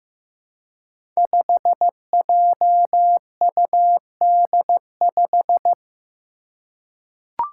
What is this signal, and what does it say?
Morse code, heard as a keyed tone, about 15 wpm, 700 Hz. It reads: 5JUD5
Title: Five Challenging Letter-Number Combinations 15wpm